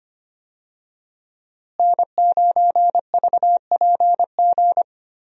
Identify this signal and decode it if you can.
Morse code, heard as a keyed tone, about 25 wpm, 700 Hz. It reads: N9VPG
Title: USA Call Signs - 5 Characters - Repeated 2x - 25wpm